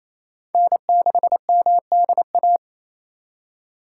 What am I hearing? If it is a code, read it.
Morse code, heard as a keyed tone, about 28 wpm, 700 Hz. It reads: N6MDA
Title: USA Call Signs - 5 Characters - 4x Character Spacing - 28wpm